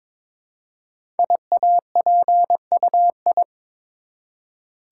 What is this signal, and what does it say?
Morse code, heard as a keyed tone, about 22 wpm, 700 Hz. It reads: IAPUI
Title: Code Groups - 5 Letters - 22wpm